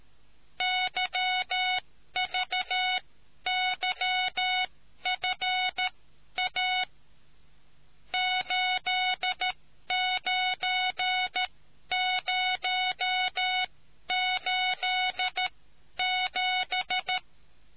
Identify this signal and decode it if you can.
Morse code, heard as a keyed tone, about 13 wpm, spaced as for 11 wpm, 740 Hz. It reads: YVYFA 89087